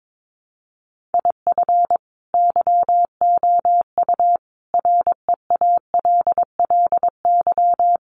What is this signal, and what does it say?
Morse code, heard as a keyed tone, about 22 wpm, 700 Hz. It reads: IF YOU REALLY
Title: Sets of 5 Words Encore2 22wpm